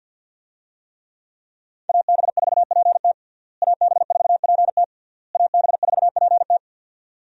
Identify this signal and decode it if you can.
Morse code, heard as a keyed tone, about 52 wpm, 700 Hz. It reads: AB4PT AB4PT AB4PT
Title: USA Call Signs - 5 Characters - Repeated 3x - 50wpm